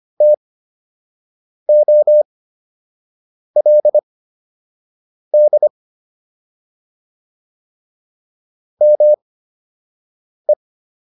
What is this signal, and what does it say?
Morse code, heard as a keyed tone, about 25 wpm, spaced as for 6 wpm, 600 Hz. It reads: TOLD ME